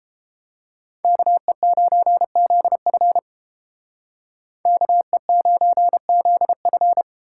Morse code, heard as a keyed tone, about 30 wpm, 700 Hz. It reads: KE9ZF KE9ZF